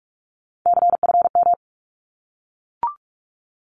Morse code, heard as a keyed tone, about 45 wpm, 700 Hz. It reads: CFM
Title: Abbreviations 45wpm